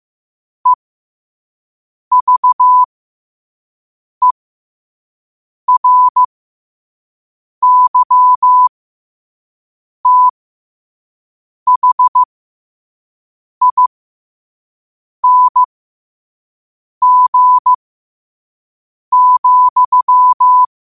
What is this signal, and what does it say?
Morse code, heard as a keyed tone, about 15 wpm, spaced as for 5 wpm, 1000 Hz. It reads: EVERYTHING,
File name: qotdfiles / 5wpm